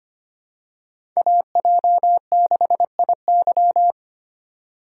Morse code, heard as a keyed tone, about 25 wpm, 700 Hz. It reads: AJ6IY
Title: USA Call Signs - 5 Characters - 3x Character Spacing - 25wpm